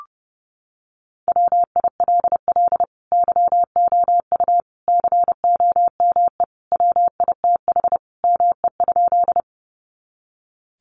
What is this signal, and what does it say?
Morse code, heard as a keyed tone, about 30 wpm, 700 Hz. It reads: WILL YOU COME WITH ME?
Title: Sentences from Top 200 Words 30wpm